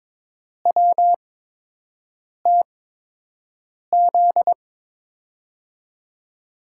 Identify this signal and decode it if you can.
Morse code, heard as a keyed tone, about 22 wpm, 700 Hz. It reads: W T Z
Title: USA Call Signs - 6 Characters - 8x Character Spacing - 22wpm